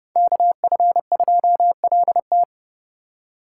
Morse code, heard as a keyed tone, about 38 wpm, 700 Hz. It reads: KF2LT